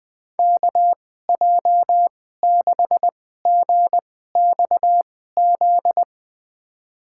Morse code, heard as a keyed tone, about 20 wpm, spaced as for 15 wpm, 700 Hz. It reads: KJ6GXZ
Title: USA Call Signs - 6 Characters - 2x Character Spacing - 20wpm